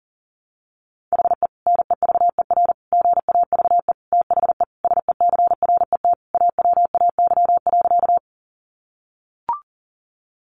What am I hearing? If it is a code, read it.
Morse code, heard as a keyed tone, about 40 wpm, 700 Hz. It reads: HE NEVER GAVE THE SECRET AWAY.
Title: Sentences from Top 1200 Words 40wpm